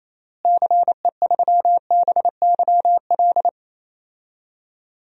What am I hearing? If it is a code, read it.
Morse code, heard as a keyed tone, about 27 wpm, 700 Hz. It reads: CE3BYL
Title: DXCC-100 Call Signs - 6 Characters - Repeated 3x - 28wpm